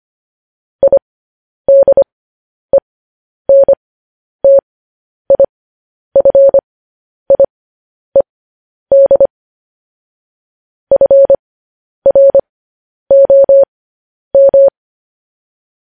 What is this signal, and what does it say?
Morse code, heard as a keyed tone, about 25 wpm, spaced as for 10 wpm, 550 Hz. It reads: IDENTIFIED FROM
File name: podcast / News headlines-1x-25x10